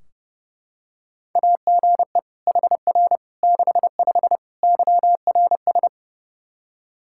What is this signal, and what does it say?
Morse code, heard as a keyed tone, about 32 wpm, 700 Hz. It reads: AGE HR 65 YRS
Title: Practice QSO - Exchange 3 - More Info - 30wpm